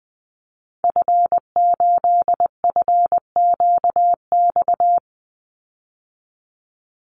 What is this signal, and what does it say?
Morse code, heard as a keyed tone, about 20 wpm, 700 Hz. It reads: F8FQX
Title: Call Signs Repeated 3x - 20wpm